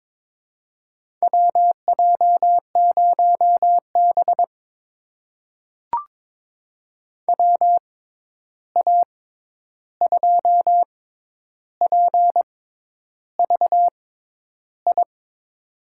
Morse code, heard as a keyed tone, about 22 wpm, spaced as for 10 wpm, 700 Hz. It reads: WJ0B WA2PVI
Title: USA Call Signs - 6x Character Spacing - 22wpm